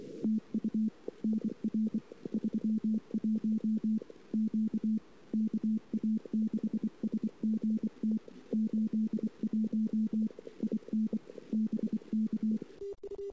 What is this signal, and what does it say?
Morse code, heard as a keyed tone, about 24 wpm, 220 Hz. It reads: TU NR 31 Q KA6SGT 81 IN BK
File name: CW / 2008-ss-cw-k5tr-op-n5rz -22:00